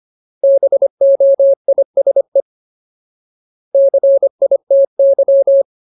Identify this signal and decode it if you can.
Morse code, heard as a keyed tone, about 25 wpm, 550 Hz. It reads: BOISE CITY
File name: Ultimate Word List / 25WPM Ultimate